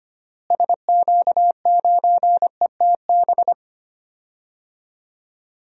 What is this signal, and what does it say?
Morse code, heard as a keyed tone, about 25 wpm, 700 Hz. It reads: SQ9ETB
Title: DXCC-100 Call Signs - 6 Characters - Repeated 2x - 25wpm